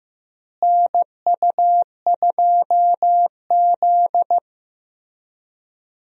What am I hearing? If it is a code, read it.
Morse code, heard as a keyed tone, about 15 wpm, 700 Hz. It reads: NU2Z
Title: USA Call Signs - 4 Characters - Repeated 3x - 15wpm